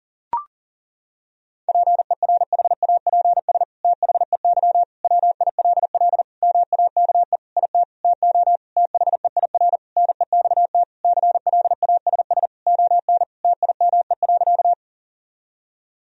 Morse code, heard as a keyed tone, about 37 wpm, 700 Hz. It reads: PERHAPS THEY WILL MAKE IT TO THEIR NEXT CLASS ON TIME.
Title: Sentences from Top 1000 Words - Review Part 2 - 40wpm